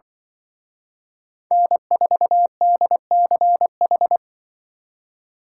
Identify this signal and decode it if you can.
Morse code, heard as a keyed tone, about 25 wpm, 700 Hz. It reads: N4DCH
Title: Speed Racing - USA 1x3 Call Signs 20wpm